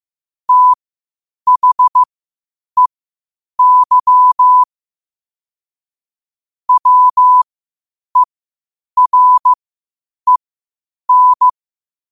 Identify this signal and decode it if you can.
Morse code, heard as a keyed tone, about 15 wpm, spaced as for 8 wpm, 1000 Hz. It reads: THEY WEREN
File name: qotdfiles / 8wpm